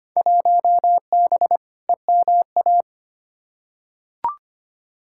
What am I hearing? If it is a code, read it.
Morse code, heard as a keyed tone, about 25 wpm, 700 Hz. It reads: 1B EMA
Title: Field Day Exchanges - 25wpm